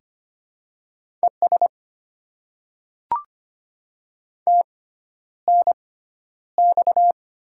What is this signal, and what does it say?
Morse code, heard as a keyed tone, about 25 wpm, spaced as for 11 wpm, 700 Hz. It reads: ES TNX
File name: Top 20 QSO Elements - 6x Character Spacing - 25wpm